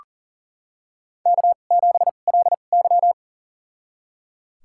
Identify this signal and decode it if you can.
Morse code, heard as a keyed tone, about 40 wpm, spaced as for 29 wpm, 700 Hz. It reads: K7LY
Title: USA Call Signs - 4 Characters - 2x Character Spacing - 40wpm